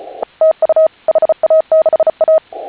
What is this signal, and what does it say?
Morse code, heard as a keyed tone, about 34 wpm, 630 Hz. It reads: TU HA6A